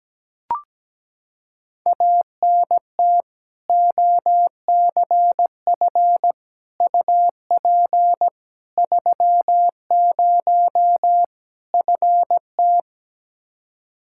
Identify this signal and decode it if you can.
Morse code, heard as a keyed tone, about 17 wpm, 700 Hz. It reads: ANT OCF UP 30 FT
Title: Practice QSO - Exchange 2 - Info - 17wpm